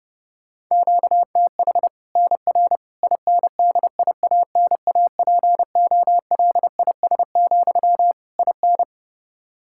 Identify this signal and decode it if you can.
Morse code, heard as a keyed tone, about 32 wpm, 700 Hz. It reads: QTH NR INDIANAPOLIS, IN
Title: Practice QSO - Exchange 1 - Beginning - 30wpm